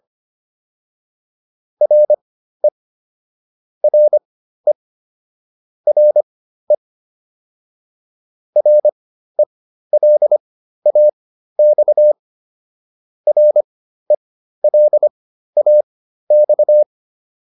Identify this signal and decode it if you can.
Morse code, heard as a keyed tone, about 25 wpm, spaced as for 13 wpm, 600 Hz. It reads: RE RE RE RELAX RELAX